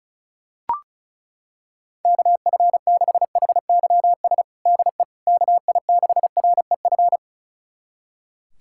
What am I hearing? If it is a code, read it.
Morse code, heard as a keyed tone, about 36 wpm, 690 Hz. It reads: KF6HYS DE KI6REF
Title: USA Callsign DE Callsign - 6 Characters - 35wpm